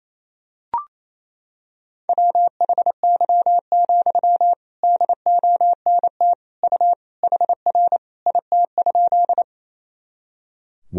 Morse code, heard as a keyed tone, about 28 wpm, 700 Hz. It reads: WHY, DONT U HR IT?
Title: Sentences from Top 300 Words and Abbreviations 28wpm